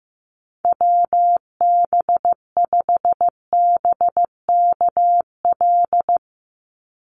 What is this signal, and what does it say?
Morse code, heard as a keyed tone, about 15 wpm, 700 Hz. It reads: WB5BKL
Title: NA SOTA Chaser Callsigns 15wpm